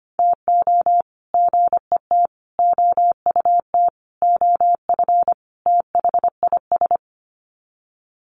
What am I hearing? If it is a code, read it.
Morse code, heard as a keyed tone, about 25 wpm, 700 Hz. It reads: TO GET OUT OF THIS
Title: Sets of 5 Words Encore2 25wpm